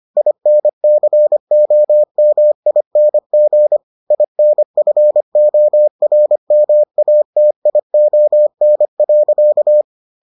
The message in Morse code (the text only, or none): INCOMING INFORMATION.